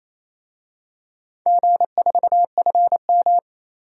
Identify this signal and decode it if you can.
Morse code, heard as a keyed tone, about 28 wpm, 700 Hz. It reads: G4FM
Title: DXCC-100 Call Signs - 4 Characters - Repeated 2x - 28wpm